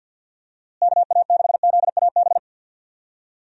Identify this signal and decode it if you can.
Morse code, heard as a keyed tone, about 52 wpm, 700 Hz. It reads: KA6ZAB
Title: USA Call Signs - 6 Characters - Repeated 2x - 50wpm